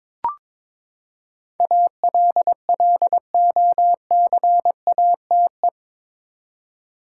Morse code, heard as a keyed tone, about 22 wpm, 700 Hz. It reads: ALLOCATE
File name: practice / 22wpm / Eight Letter Words - 22wpm